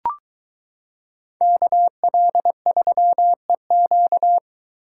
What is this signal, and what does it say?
Morse code, heard as a keyed tone, about 23 wpm, 700 Hz. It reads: KL3EQ